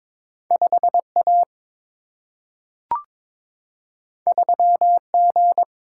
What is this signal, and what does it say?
Morse code, heard as a keyed tone, about 22 wpm, 700 Hz. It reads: 5A 3G
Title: Single-Letter Single-Number 22wpm